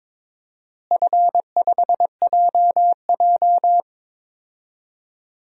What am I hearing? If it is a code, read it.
Morse code, heard as a keyed tone, about 22 wpm, 700 Hz. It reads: F5JJ
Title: DXCC-100 Call Signs - 4 Characters - Repeated 3x - 22wpm